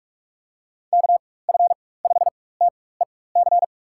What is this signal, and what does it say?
Morse code, heard as a keyed tone, about 46 wpm, spaced as for 21 wpm, 700 Hz. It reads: KF5TEC
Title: USA Call Signs - 4x Character Spacing - 45wpm